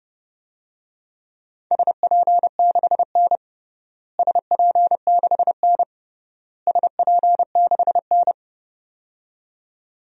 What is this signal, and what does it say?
Morse code, heard as a keyed tone, about 30 wpm, 700 Hz. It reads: SP6N SP6N SP6N
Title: DXCC-100 Call Signs - 4 Characters - Repeated 3x - 30wpm